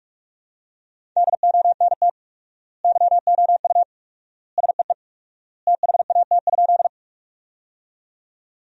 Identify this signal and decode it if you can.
Morse code, heard as a keyed tone, about 46 wpm, 700 Hz. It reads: DONT YOU SEE THAT?
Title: 4x Word-Spacing - Sentences from Top 300 Words - 45wpm